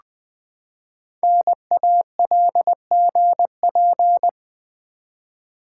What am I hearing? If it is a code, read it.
Morse code, heard as a keyed tone, about 20 wpm, 700 Hz. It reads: NALGP
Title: Code Groups - 5 Letters - 20wpm